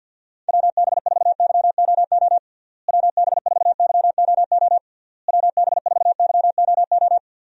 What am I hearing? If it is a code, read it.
Morse code, heard as a keyed tone, about 52 wpm, 700 Hz. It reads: WB4YOO WB4YOO WB4YOO
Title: USA Call Signs - 6 Characters - Repeated 3x - 50wpm